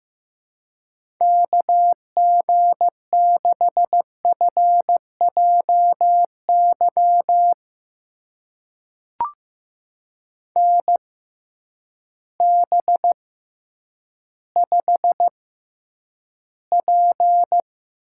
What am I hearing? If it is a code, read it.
Morse code, heard as a keyed tone, about 15 wpm, spaced as for 9 wpm, 700 Hz. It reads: KG6FJY NB5P
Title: USA Call Signs - 6x Character Spacing - 15wpm